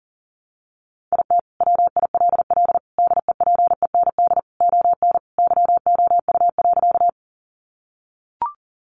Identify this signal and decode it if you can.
Morse code, heard as a keyed tone, about 37 wpm, 700 Hz. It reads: IT WILL DEPEND ON YOU.
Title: Sentences from Top 1300 Words 40wpm